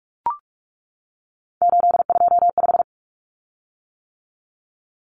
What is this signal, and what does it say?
Morse code, heard as a keyed tone, about 44 wpm, 700 Hz. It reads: 825
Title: Common Cut Numbers - 3-Digit 45wpm